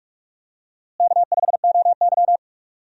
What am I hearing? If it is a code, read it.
Morse code, heard as a keyed tone, about 46 wpm, 700 Hz. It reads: K5OY
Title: USA Call Signs - 4 Characters - Repeated 2x - 45wpm